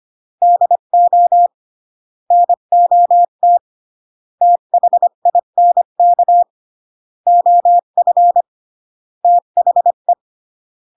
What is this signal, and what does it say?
Morse code, heard as a keyed tone, about 25 wpm, 700 Hz. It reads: DO NOT THINK OF THE